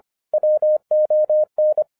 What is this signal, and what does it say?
Morse code, heard as a keyed tone, about 25 wpm, 610 Hz. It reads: WON
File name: 2011-07-mp3 / n39 THE 3.11 2 25wpm